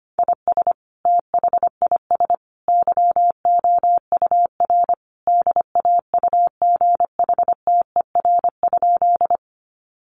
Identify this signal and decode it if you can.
Morse code, heard as a keyed tone, about 25 wpm, 700 Hz. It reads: IS THIS YOUR DAUGHTER?
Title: Sentences from Top 1500 Words 25wpm